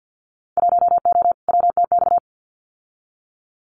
Morse code, heard as a keyed tone, about 50 wpm, 700 Hz. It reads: 1O WTX
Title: Winter Field Day Exchanges Encore 50wpm